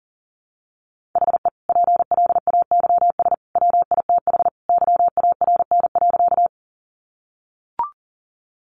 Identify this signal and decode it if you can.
Morse code, heard as a keyed tone, about 40 wpm, 700 Hz. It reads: HE PLAYS WITH YARN.
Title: Easy Phrases 40wpm